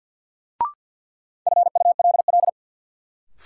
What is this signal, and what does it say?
Morse code, heard as a keyed tone, about 53 wpm, 700 Hz. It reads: FULL
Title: Sentences from Top 500 Words - New Word Drill - 50wpm